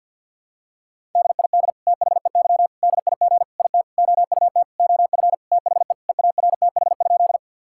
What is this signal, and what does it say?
Morse code, heard as a keyed tone, about 50 wpm, 700 Hz. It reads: DID THEY DIG IT OUT OF THE EARTH?
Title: Sentences from Top 1600 Words 50wpm